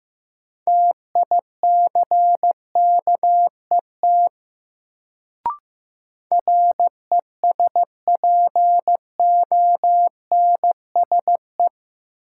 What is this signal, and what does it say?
Morse code, heard as a keyed tone, about 15 wpm, 700 Hz. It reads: TICKET RESPONSE
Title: Sentences from Top 1400 Words - New Word Drill - 15wpm